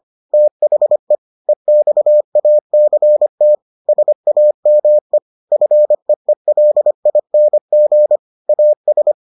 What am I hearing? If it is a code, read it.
Morse code, heard as a keyed tone, about 25 wpm, 600 Hz. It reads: THE EXACT SAME FEELING AS